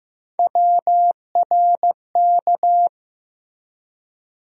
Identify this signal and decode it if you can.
Morse code, heard as a keyed tone, about 15 wpm, 700 Hz. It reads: WRK
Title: USA Call Sign Postfixes - 3 Letters 15wpm